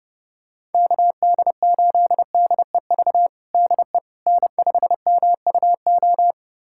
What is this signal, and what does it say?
Morse code, heard as a keyed tone, about 30 wpm, 700 Hz. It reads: KD8DEV DE N5MUO
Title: USA Callsign DE Callsign 30wpm